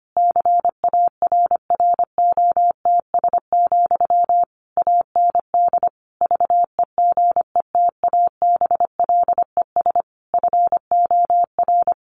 Morse code, heard as a keyed tone, about 25 wpm, 700 Hz. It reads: CARROTS, AND VEGETABLES FOR
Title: Easy Phrases 25wpm